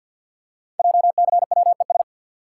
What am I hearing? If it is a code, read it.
Morse code, heard as a keyed tone, about 50 wpm, 690 Hz. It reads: JCWES
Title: Code Groups - 5 Letters - 50wpm